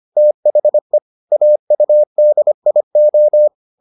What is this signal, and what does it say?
Morse code, heard as a keyed tone, about 25 wpm, 600 Hz. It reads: THE AUDIO